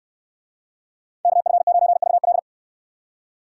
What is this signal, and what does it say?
Morse code, heard as a keyed tone, about 64 wpm, 700 Hz. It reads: KF9VL